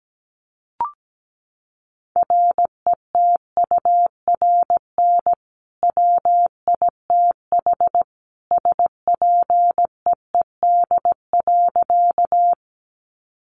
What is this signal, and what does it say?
Morse code, heard as a keyed tone, about 17 wpm, 700 Hz. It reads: RETURN WITH SPEED.
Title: Sentences from Top 800 Words 17wpm